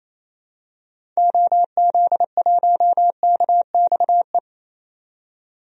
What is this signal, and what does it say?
Morse code, heard as a keyed tone, about 28 wpm, 700 Hz. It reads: OZ1KXE
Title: DXCC-100 Call Signs - 6 Characters - 28wpm